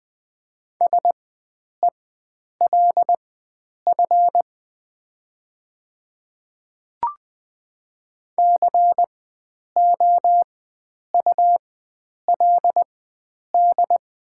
Morse code, heard as a keyed tone, about 20 wpm, spaced as for 9 wpm, 700 Hz. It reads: SELF COULD
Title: 4x Character Spacing - Top 300 Words - 20wpm